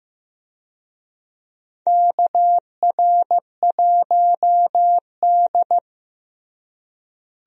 Morse code, heard as a keyed tone, about 15 wpm, 700 Hz. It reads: KR1D
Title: USA Call Signs - 4 Characters - Repeated 3x - 15wpm